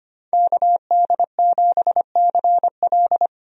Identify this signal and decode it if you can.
Morse code, heard as a keyed tone, about 25 wpm, 690 Hz. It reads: KD7CL
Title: USA Call Signs - 5 Characters - 25wpm